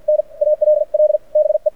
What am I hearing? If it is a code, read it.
Morse code, heard as a keyed tone, about 52 wpm, spaced as for 45 wpm, 600 Hz. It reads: N ALL BE
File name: archived / CRSnet